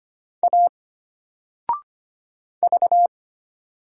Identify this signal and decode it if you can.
Morse code, heard as a keyed tone, about 25 wpm, 700 Hz. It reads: A V
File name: Single-Letter-Number-25wpm